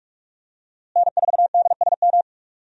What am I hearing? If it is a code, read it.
Morse code, heard as a keyed tone, about 46 wpm, 700 Hz. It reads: N4DSM
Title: USA Call Signs - 5 Characters - Repeated 3x - 45wpm